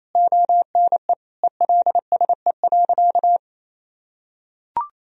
Morse code, heard as a keyed tone, about 28 wpm, 700 Hz. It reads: ONE ELSE.
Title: Sentences from Top 1000 Words - Review Part 1 - 28wpm